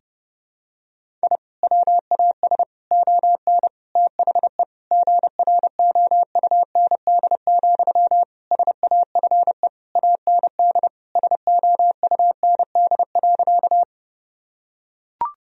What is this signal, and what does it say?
Morse code, heard as a keyed tone, about 30 wpm, 700 Hz. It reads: I WAS ON THE GROUND, SAFE AND SOUND.
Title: Sentences from Top 1000 Words - Review Part 2 - 30wpm